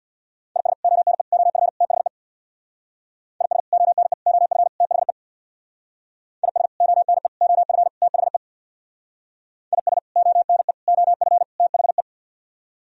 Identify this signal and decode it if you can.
Morse code, heard as a keyed tone, about 63 wpm, 700 Hz. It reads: IS ONE OF THE IS ONE OF THE IS ONE OF THE IS ONE OF THE